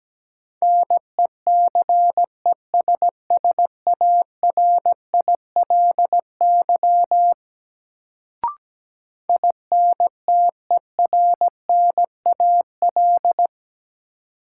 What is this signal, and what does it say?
Morse code, heard as a keyed tone, about 17 wpm, 700 Hz. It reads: NECESSARILY INTERNAL
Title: Sentences from Top 1400 Words - New Word Drill Alt - 17wpm